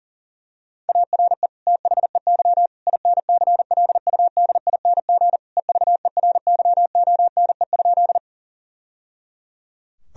Morse code, heard as a keyed tone, about 40 wpm, 700 Hz. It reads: ARE THEY INCLUDING EVERYONE?
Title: Sentences from Top 1000 Words - Review Part 4 - 40wpm